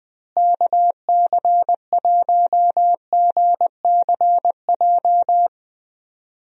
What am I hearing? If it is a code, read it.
Morse code, heard as a keyed tone, about 20 wpm, 700 Hz. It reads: KC1GCJ
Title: USA Call Signs - 6 Characters - Repeated 3x - 20wpm